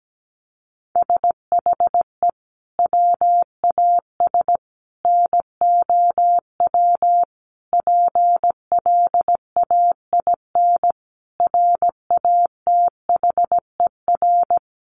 Morse code, heard as a keyed tone, about 17 wpm, 700 Hz. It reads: SHE WAS NOW PLAIN RATHER
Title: Sentences from Top 1000 Words 17wpm